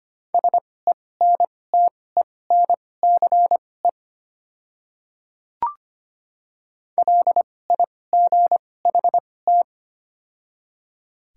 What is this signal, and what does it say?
Morse code, heard as a keyed tone, about 25 wpm, spaced as for 18 wpm, 700 Hz. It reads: SENTENCE LIGHT
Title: 2x Character Spacing - Top 200 Words - 25wpm